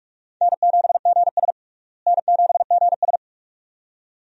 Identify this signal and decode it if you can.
Morse code, heard as a keyed tone, about 41 wpm, 700 Hz. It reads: N7GS N7GS
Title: USA Call Signs - 4 Characters - Repeated 2x - 45wpm